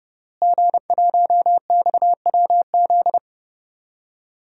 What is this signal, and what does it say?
Morse code, heard as a keyed tone, about 30 wpm, 700 Hz. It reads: G1XWZ
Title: DXCC-100 Call Signs - 5 Characters - Repeated 3x - 30wpm